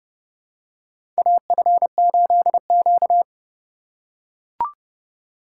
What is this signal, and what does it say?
Morse code, heard as a keyed tone, about 30 wpm, 700 Hz. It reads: AF8Q
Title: USA Call Signs - 4x Character Spacing - 30wpm